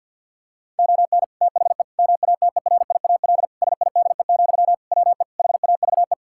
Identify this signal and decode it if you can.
Morse code, heard as a keyed tone, about 52 wpm, 700 Hz. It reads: ON THE MATERIAL SIDE, WE HAVE
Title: Sentences from Top 1000 Words - Review Part 5 - 50wpm